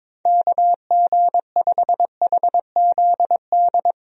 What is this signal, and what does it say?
Morse code, heard as a keyed tone, about 22 wpm, 700 Hz. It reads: KG5HZD
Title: USA Call Signs - 6 Characters - 22wpm